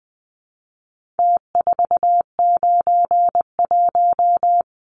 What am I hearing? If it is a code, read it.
Morse code, heard as a keyed tone, about 20 wpm, 700 Hz. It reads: T491
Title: Numbers - Contest Style - 4-Digit 20wpm